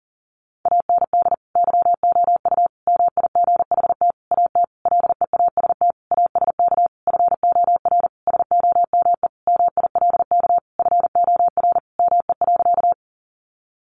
Morse code, heard as a keyed tone, about 40 wpm, 700 Hz. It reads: AND YOU MIGHT AT LEAST ASK FOR SOME MILK FOR ME.
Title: Sentences from Top 1500 Words 40wpm